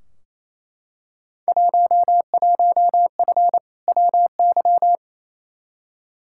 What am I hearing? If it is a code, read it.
Morse code, heard as a keyed tone, about 28 wpm, 700 Hz. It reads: 11F WY
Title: Field Day Exchanges Encore - 28wpm